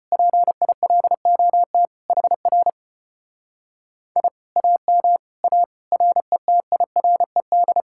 Morse code, heard as a keyed tone, about 33 wpm, 700 Hz. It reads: PILOT HR I AM A RETIRED